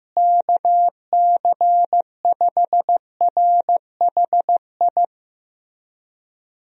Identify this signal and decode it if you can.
Morse code, heard as a keyed tone, about 15 wpm, 700 Hz. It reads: KC5RHI
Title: USA Call Signs - 6 Characters - Repeated 3x - 15wpm